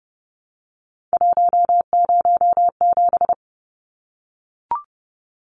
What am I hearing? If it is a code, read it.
Morse code, heard as a keyed tone, about 30 wpm, 700 Hz. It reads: 107
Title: Numbers - 3-Digit 30wpm